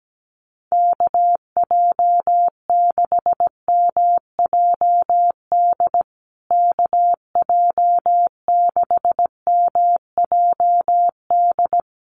Morse code, heard as a keyed tone, about 17 wpm, 700 Hz. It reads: KJ6MJD KJ6MJD
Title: USA Call Signs Repeated 3x 17wpm